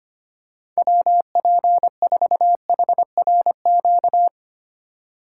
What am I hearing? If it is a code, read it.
Morse code, heard as a keyed tone, about 25 wpm, 700 Hz. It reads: WP4HRQ
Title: USA Call Signs - 2x Character Spacing - 25wpm